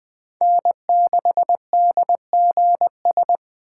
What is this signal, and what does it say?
Morse code, heard as a keyed tone, about 21 wpm, 700 Hz. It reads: N6DGS
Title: USA Call Signs - 5 Characters - 6x Character Spacing - 20wpm